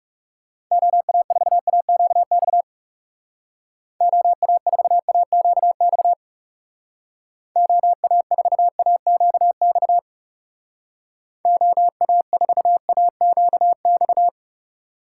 Morse code, heard as a keyed tone, about 36 wpm, 700 Hz. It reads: OA4AQX OA4AQX OA4AQX OA4AQX
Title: Speed Racing - USA 2x3 Call Signs 30wpm